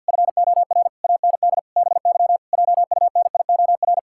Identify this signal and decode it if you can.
Morse code, heard as a keyed tone, about 53 wpm, 700 Hz. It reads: FOR AND BY JUNIOR
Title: Pangram Challenge v2 - 50wpm